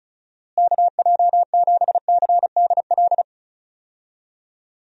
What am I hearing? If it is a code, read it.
Morse code, heard as a keyed tone, about 34 wpm, 700 Hz. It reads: KJ7CDL
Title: USA Call Signs - 6 Characters - Repeated 3x - 35wpm